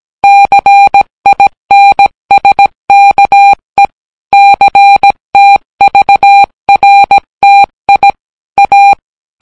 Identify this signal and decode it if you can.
Morse code, heard as a keyed tone, about 17 wpm, 800 Hz. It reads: CINSKE CTVRTI A